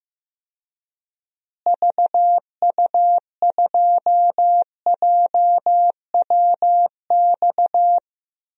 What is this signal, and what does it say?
Morse code, heard as a keyed tone, about 15 wpm, 700 Hz. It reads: VU2JWX
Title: DXCC-100 Call Signs - 6 Characters - Repeated 2x - 15wpm